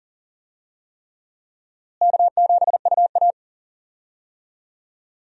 Morse code, heard as a keyed tone, about 39 wpm, 700 Hz. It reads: K7UA
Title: USA Call Signs - 4 Characters - Repeated 3x - 40wpm